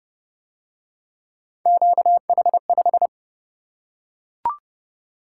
Morse code, heard as a keyed tone, about 31 wpm, 700 Hz. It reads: QH5